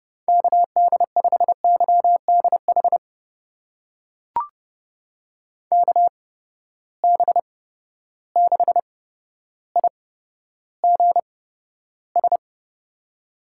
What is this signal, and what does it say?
Morse code, heard as a keyed tone, about 30 wpm, spaced as for 13 wpm, 700 Hz. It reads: KD5YDH KB6IGS
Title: USA Call Signs - 6 Characters - 8x Character Spacing - 30wpm